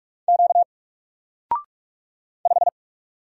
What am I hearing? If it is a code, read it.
Morse code, heard as a keyed tone, about 46 wpm, 700 Hz. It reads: Q 5
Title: Single Letter-Number 45wpm